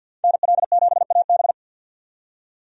Morse code, heard as a keyed tone, about 52 wpm, 700 Hz. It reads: NL7AB
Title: USA Call Signs - 5 Characters - 50wpm